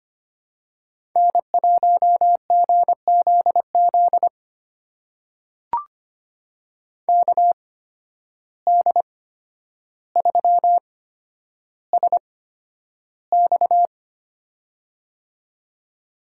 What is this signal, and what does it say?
Morse code, heard as a keyed tone, about 25 wpm, spaced as for 11 wpm, 700 Hz. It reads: N1GZZ KD3SX
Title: USA Call Signs - 5 Characters - 8x Character Spacing - 25wpm